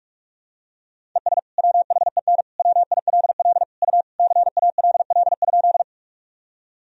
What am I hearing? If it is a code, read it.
Morse code, heard as a keyed tone, about 46 wpm, 700 Hz. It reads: ES WHEN WILL U CALL?
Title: Sentences from Top 100 Words and Abbreviations 45wpm